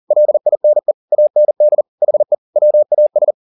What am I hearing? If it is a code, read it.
Morse code, heard as a keyed tone, about 40 wpm, 600 Hz. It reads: LINE AND HE WAS
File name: the old man and the sea 2 40wpm